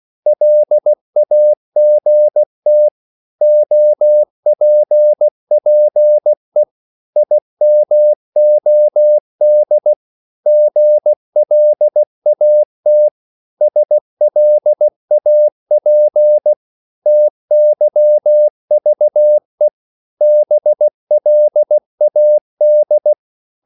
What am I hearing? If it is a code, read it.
Morse code, heard as a keyed tone, about 16 wpm, 600 Hz. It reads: LAGT OPPE IMOD GLAT SLAP TYVE BLAD
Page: Korte danske ord 16wpm